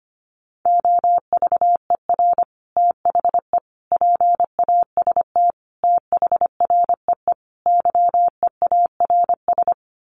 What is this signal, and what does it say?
Morse code, heard as a keyed tone, about 25 wpm, 700 Hz. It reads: OVER THE PAST THREE YEARS
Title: Sets of 5 Words Encore2 25wpm